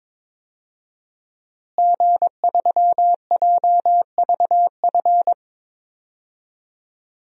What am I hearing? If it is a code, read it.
Morse code, heard as a keyed tone, about 21 wpm, 700 Hz. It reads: G3JVF
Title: DXCC-100 Call Signs - 5 Characters - Repeated 2x - 22wpm